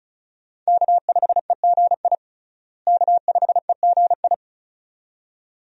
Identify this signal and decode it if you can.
Morse code, heard as a keyed tone, about 35 wpm, 700 Hz. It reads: K5EGI K5EGI
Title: USA Call Signs - 5 Characters - Repeated 2x - 35wpm